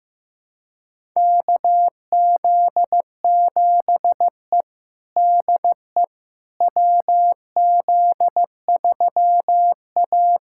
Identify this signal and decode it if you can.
Morse code, heard as a keyed tone, about 15 wpm, 700 Hz. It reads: KZ7E DE WZ3A
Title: USA Callsign DE Callsign - 4 Characters - 15wpm